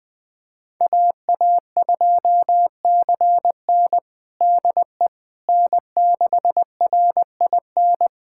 Morse code, heard as a keyed tone, about 20 wpm, 700 Hz. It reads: AA2CN DE N6RIN
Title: USA Callsign DE Callsign - 5 Characters - 20wpm